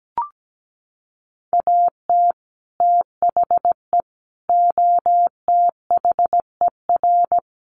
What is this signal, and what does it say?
Morse code, heard as a keyed tone, about 17 wpm, 700 Hz. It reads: AT THE OTHER
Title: Sets of 5 Words 17wpm